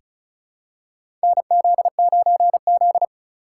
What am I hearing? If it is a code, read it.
Morse code, heard as a keyed tone, about 35 wpm, 700 Hz. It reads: NZ9Z
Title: USA Call Signs - 8x Character Spacing - 35wpm